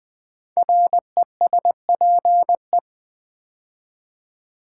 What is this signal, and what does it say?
Morse code, heard as a keyed tone, about 20 wpm, 700 Hz. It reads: RESPE
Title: Build A Word - 5 Letters At A Time - 12-Letter Words - 20wpm